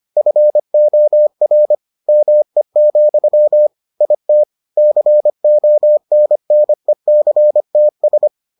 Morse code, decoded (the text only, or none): FOR ME, IT CONNECTS